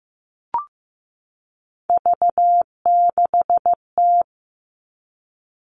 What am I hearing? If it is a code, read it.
Morse code, heard as a keyed tone, about 15 wpm, 700 Hz. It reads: V6T
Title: All Cut Numbers - 3-Digit 15wpm